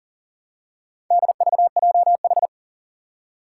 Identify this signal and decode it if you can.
Morse code, heard as a keyed tone, about 37 wpm, 700 Hz. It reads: DVJH
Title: Four Challenging Letter-Number Combinations 40wpm